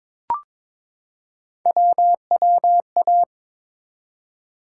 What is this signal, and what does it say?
Morse code, heard as a keyed tone, about 22 wpm, 700 Hz. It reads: WWA
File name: ARRL Sections v2 - 22wpm